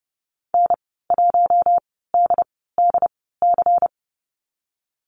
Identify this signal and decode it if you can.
Morse code, heard as a keyed tone, about 30 wpm, spaced as for 17 wpm, 700 Hz. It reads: N1DDC
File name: Lesson 19g - Letter w - Call Signs 30wpm